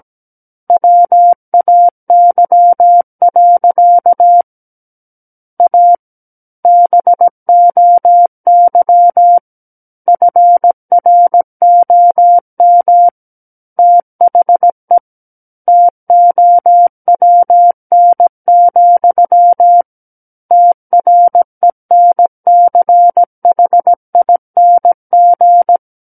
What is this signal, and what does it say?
Morse code, heard as a keyed tone, about 17 wpm, 700 Hz. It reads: WAY. A BOY FROM THE TOWN, TRENCHING
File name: Book One / 08-Chapter 8 - 17 WPM